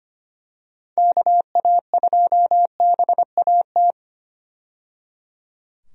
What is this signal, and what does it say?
Morse code, heard as a keyed tone, about 25 wpm, 700 Hz. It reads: KA2BAT
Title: USA Call Signs - 6 Characters - 25wpm